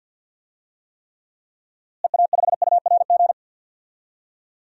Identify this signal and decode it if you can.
Morse code, heard as a keyed tone, about 49 wpm, 700 Hz. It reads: EA5URG
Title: DXCC-100 Call Signs - 6 Characters - Repeated 3x - 50wpm